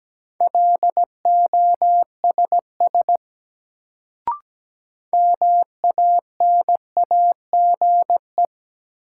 Morse code, heard as a keyed tone, about 17 wpm, 700 Hz. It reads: LOSS MANAGE
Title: Sentences from Top 1400 Words - New Word Drill - 17wpm